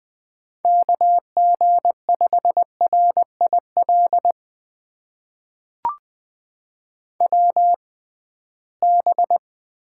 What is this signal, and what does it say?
Morse code, heard as a keyed tone, about 20 wpm, spaced as for 15 wpm, 700 Hz. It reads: KG5RIL WB
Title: USA Call Signs - 6 Characters - 6x Character Spacing - 20wpm